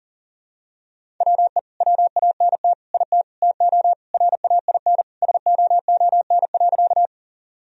Morse code, heard as a keyed tone, about 40 wpm, 700 Hz. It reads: WE WANT IT TO RAIN SOON.
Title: Sentences from Top 1000 Words - Review Part 1 - 40wpm